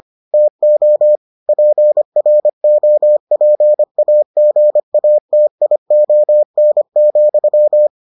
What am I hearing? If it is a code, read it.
Morse code, heard as a keyed tone, about 25 wpm, 600 Hz. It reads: TO PROPAGATION,